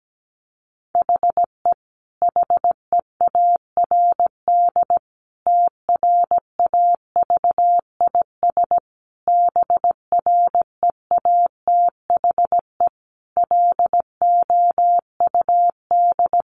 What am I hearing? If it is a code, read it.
Morse code, heard as a keyed tone, about 17 wpm, 700 Hz. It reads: HE HEARD TRAVIS BREATHE LOUD
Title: A Sound Of Thunder 17wpm